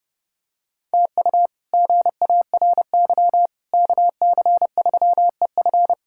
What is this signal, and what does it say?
Morse code, heard as a keyed tone, about 30 wpm, 700 Hz. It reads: TU GARY KC3EF
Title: K1USN SST QSO Practice - 30wpm